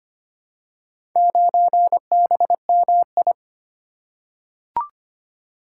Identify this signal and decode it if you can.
Morse code, heard as a keyed tone, about 25 wpm, 700 Hz. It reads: 9BMI
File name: Four Letter-Number Combinations - 25wpm